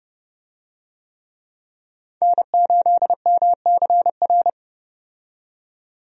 Morse code, heard as a keyed tone, about 30 wpm, 700 Hz. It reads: N8MCR
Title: USA Call Signs - 5 Characters - 30wpm